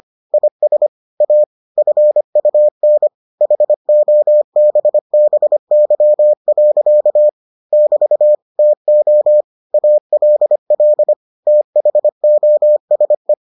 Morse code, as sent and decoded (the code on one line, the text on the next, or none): .. ... / .- / ..-. ..- -. / .... --- -... -... -.-- .-.-.- / -...- / - --- / .- .-.. .-.. / - .... --- ... .
IS A FUN HOBBY. <BT> TO ALL THOSE